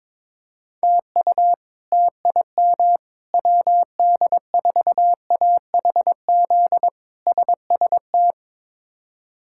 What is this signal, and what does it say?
Morse code, heard as a keyed tone, about 21 wpm, 700 Hz. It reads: TU TIM WD4AHZ SST
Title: K1USN SST QSO Practice - 22wpm